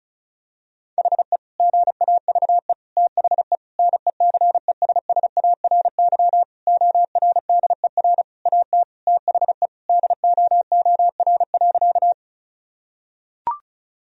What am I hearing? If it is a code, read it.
Morse code, heard as a keyed tone, about 34 wpm, 700 Hz. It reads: HE GAVE THE NECESSARY ORDER AT THE DOOR.
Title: Sentences from Top 1000 Words - Review Part 2 - 35wpm